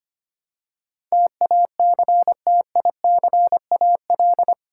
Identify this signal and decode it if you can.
Morse code, heard as a keyed tone, about 25 wpm, 700 Hz. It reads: TACTICAL
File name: Eight Letter Words - 25wpm